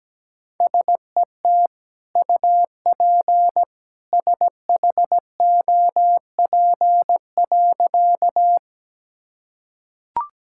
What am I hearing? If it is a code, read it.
Morse code, heard as a keyed tone, about 17 wpm, 700 Hz. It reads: SET UP SHOP.
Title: Sentences from Top 1600 Words 17wpm